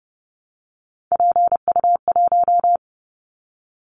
Morse code, heard as a keyed tone, about 30 wpm, 700 Hz. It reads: PU1
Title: Callsign Prefixes for Top 50 DXCC - 30wpm